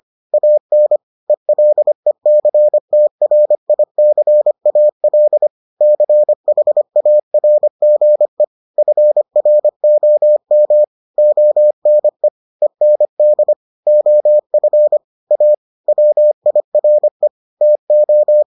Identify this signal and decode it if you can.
Morse code, heard as a keyed tone, about 25 wpm, 600 Hz. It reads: AN ELECTRICAL CHARGE FROM ONE END OF A WIRE TO